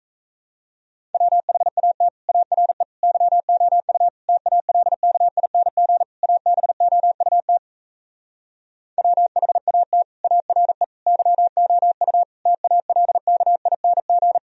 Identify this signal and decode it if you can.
Morse code, heard as a keyed tone, about 38 wpm, 700 Hz. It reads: WHAT ARE YOU TALKING ABOUT WHAT ARE YOU TALKING